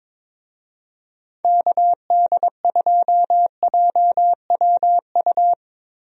Morse code, heard as a keyed tone, about 21 wpm, 700 Hz. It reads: KD2JWU
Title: USA Call Signs - 6 Characters - Repeated 3x - 22wpm